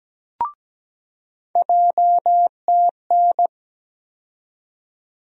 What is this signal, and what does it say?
Morse code, heard as a keyed tone, about 17 wpm, 700 Hz. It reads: JTN
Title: USA Call Sign Postfixes - 3 Letters 17wpm